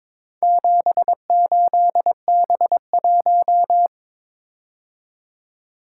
Morse code, heard as a keyed tone, about 22 wpm, 700 Hz. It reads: 78B1
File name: practice / 22wpm / Four Letter-Number Combinations - 22wpm